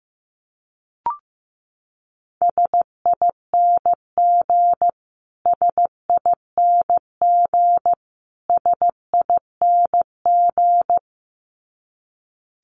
Morse code, heard as a keyed tone, about 15 wpm, 700 Hz. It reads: SING SING SING
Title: Top 500 Words Repeated 15wpm